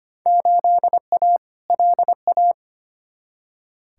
Morse code, heard as a keyed tone, about 25 wpm, 700 Hz. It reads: 8A LA
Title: Field Day Exchanges Encore - 25wpm